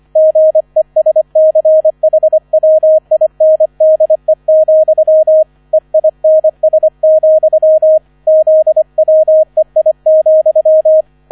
Morse code, heard as a keyed tone, about 25 wpm, 630 Hz, with radio noise. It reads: GESCHWINDE, EINS, ZWEI,
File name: 001 Erster Streich 25wpm